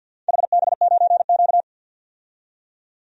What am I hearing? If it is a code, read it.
Morse code, heard as a keyed tone, about 47 wpm, 700 Hz. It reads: HB9Q
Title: DXCC-100 Call Signs - 4 Characters - 50wpm